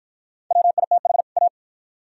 Morse code, heard as a keyed tone, about 51 wpm, 700 Hz. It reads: WITH A